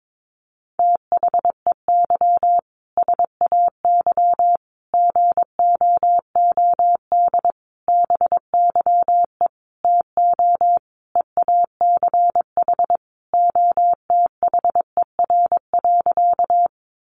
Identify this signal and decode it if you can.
Morse code, heard as a keyed tone, about 22 wpm, 700 Hz. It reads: THEY SAY GOOD BYE TO EACH OTHER.
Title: Easy Phrases 22wpm